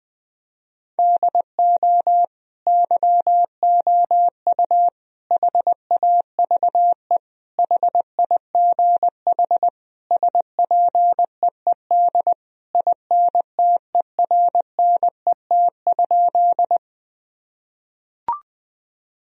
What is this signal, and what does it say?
Morse code, heard as a keyed tone, about 20 wpm, 700 Hz. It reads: DO YOU HAVE HIGH SPEED INTERNET?
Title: Sentences from Top 1000 Words - Review Part 4 - 20wpm